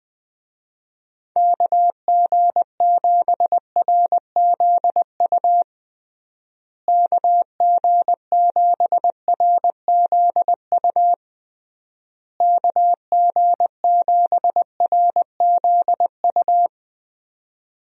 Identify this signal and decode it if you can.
Morse code, heard as a keyed tone, about 20 wpm, 700 Hz. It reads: KG7RZU KG7RZU KG7RZU
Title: USA Call Signs - 6 Characters - Repeated 3x - 20wpm